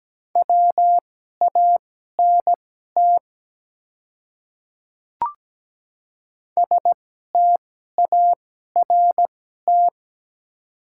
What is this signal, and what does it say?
Morse code, heard as a keyed tone, about 17 wpm, spaced as for 12 wpm, 700 Hz. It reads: WANT START
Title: 2x Character Spacing - Top 300 Words - 17wpm